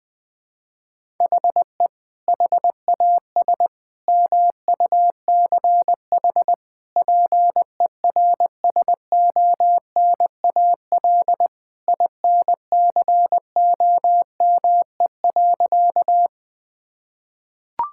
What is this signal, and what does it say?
Morse code, heard as a keyed tone, about 20 wpm, 700 Hz. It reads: HE HAS MUCH PERSONAL INCOME.
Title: Sentences from Top 1000 Words - Review Part 5 - 20wpm